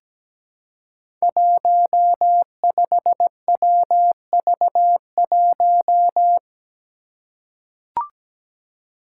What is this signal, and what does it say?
Morse code, heard as a keyed tone, about 17 wpm, 700 Hz. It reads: 15WV1
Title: Five Challenging Letter-Number Combinations 17wpm